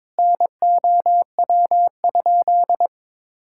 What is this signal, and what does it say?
Morse code, heard as a keyed tone, about 22 wpm, 700 Hz. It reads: NOW?
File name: Sentences with 7-Word Phrases - 22wpm